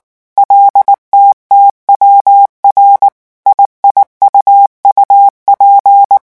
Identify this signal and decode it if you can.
Morse code, heard as a keyed tone, about 19 wpm, 800 Hz. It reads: LTTWR IIUUP
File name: morse mp3 / revision7-vitesse 18 mots